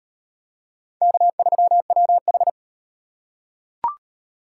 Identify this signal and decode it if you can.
Morse code, heard as a keyed tone, about 38 wpm, 700 Hz. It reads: K3WH